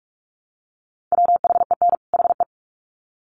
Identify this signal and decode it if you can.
Morse code, heard as a keyed tone, about 44 wpm, 700 Hz. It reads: WHEN HE
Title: Sets of 2 Words 45wpm